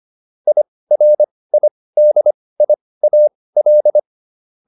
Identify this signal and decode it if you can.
Morse code, heard as a keyed tone, about 25 wpm, spaced as for 18 wpm, 600 Hz. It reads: IRIDIAL
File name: IR7 18